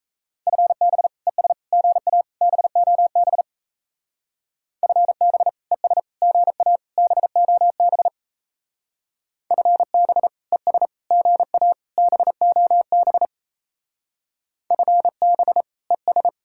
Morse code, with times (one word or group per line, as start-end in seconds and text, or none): 0.5-1.1 FB
1.3-1.5 ES
1.7-2.2 GA
2.4-3.4 BOB
4.8-5.5 FB
5.7-6.0 ES
6.2-6.8 GA
7.0-8.1 BOB
9.5-10.3 FB
10.5-10.9 ES
11.1-11.7 GA
12.0-13.3 BOB
14.7-15.6 FB
15.9-16.3 ES